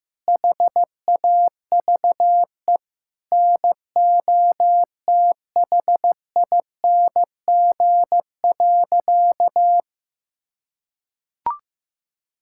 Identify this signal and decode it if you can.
Morse code, heard as a keyed tone, about 16 wpm, 700 Hz. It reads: HAVE NOTHING.
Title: Sentences from Top 1000 Words - Review Part 5 - 15wpm